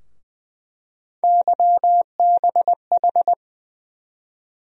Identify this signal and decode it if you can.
Morse code, heard as a keyed tone, about 20 wpm, 700 Hz. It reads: YBH
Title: USA Call Sign Postfixes - 3 Letters 20wpm